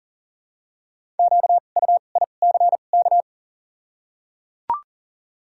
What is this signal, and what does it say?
Morse code, heard as a keyed tone, about 38 wpm, 700 Hz. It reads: Q U I C K
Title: 2x Character Spacing - Top 500 Words - 40wpm